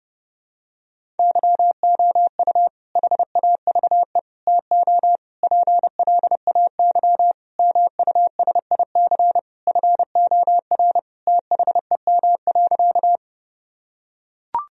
Morse code, heard as a keyed tone, about 30 wpm, 700 Hz. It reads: YOU HAVE TO PLAY MUSIC FOR THEM.
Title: Sentences from Top 1000 Words - Review Part 1 - 30wpm